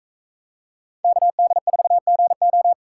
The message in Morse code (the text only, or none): KD4GO